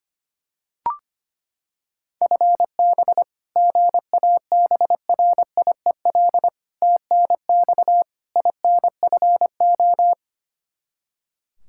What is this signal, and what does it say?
Morse code, heard as a keyed tone, about 25 wpm, 700 Hz. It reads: FB GABRIEL TNX INFO
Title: Practice QSO - Exchange 3 - More Info - 25wpm